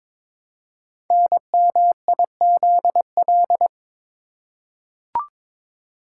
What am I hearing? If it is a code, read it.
Morse code, heard as a keyed tone, about 22 wpm, 700 Hz. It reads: NMIZL
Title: Code Groups - 5 Letters - 22wpm